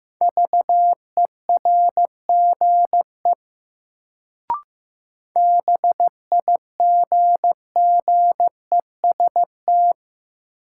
Morse code, heard as a keyed tone, about 15 wpm, 700 Hz. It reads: VERGE BIGGEST
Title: Sets of 4 Words - New Word Drill - 15wpm